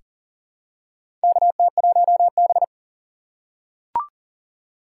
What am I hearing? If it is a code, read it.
Morse code, heard as a keyed tone, about 40 wpm, 700 Hz. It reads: KT1B
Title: K1USN SST QSO Practice - 40wpm